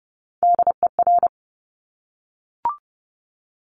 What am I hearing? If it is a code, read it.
Morse code, heard as a keyed tone, about 30 wpm, 700 Hz. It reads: DER
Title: Top Three Letter Combinations in Words 30wpm